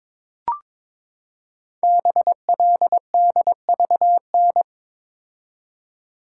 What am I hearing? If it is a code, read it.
Morse code, heard as a keyed tone, about 22 wpm, 700 Hz. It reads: BLDVN
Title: Code Groups - 5 Letter:Numbers - 22wpm